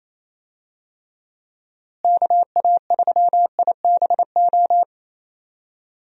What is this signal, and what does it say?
Morse code, heard as a keyed tone, about 28 wpm, 700 Hz. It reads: KA3IBO
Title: USA Call Signs - 6 Characters - Repeated 2x - 28wpm